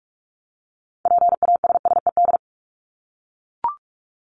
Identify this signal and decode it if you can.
Morse code, heard as a keyed tone, about 44 wpm, 700 Hz. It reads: PASSED
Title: ED Suffix - Top 100 Words with Base Words 45wpm